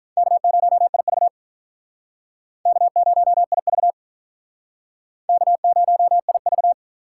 Morse code, heard as a keyed tone, about 47 wpm, 700 Hz. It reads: K0IV K0IV K0IV